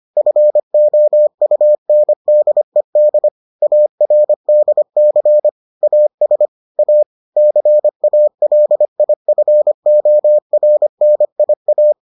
Morse code, heard as a keyed tone, about 25 wpm, 600 Hz. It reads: FOUNDED ARDC AS A CALIFORNIA